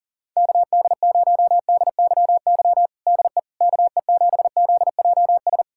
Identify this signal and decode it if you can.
Morse code, heard as a keyed tone, about 40 wpm, 700 Hz. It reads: KD0DYY DE KE7ZJS
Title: USA Callsign DE Callsign - 6 Characters - 40wpm